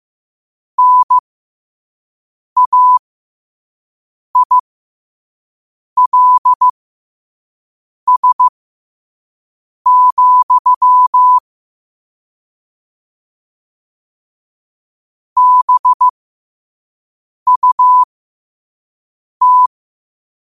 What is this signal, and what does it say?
Morse code, heard as a keyed tone, about 15 wpm, spaced as for 5 wpm, 1000 Hz. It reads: NAILS, BUT
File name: qotdfiles / 5wpm